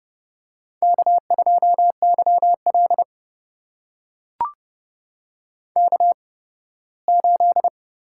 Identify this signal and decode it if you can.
Morse code, heard as a keyed tone, about 30 wpm, spaced as for 18 wpm, 700 Hz. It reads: K2YL K8
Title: USA Call Signs - 4 Characters - 8x Character Spacing - 30wpm